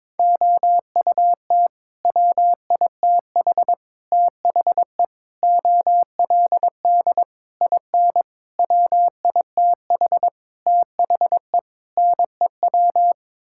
Morse code, heard as a keyed tone, about 22 wpm, 700 Hz. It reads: OUT WITH THE OLD IN WITH THE NEW
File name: practice / 22wpm / Sets of 8 Words - 22wpm